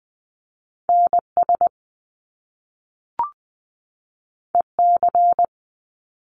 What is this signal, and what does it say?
Morse code, heard as a keyed tone, about 20 wpm, 700 Hz. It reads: NS EC
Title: Top Two Letter Combinations in Words 20wpm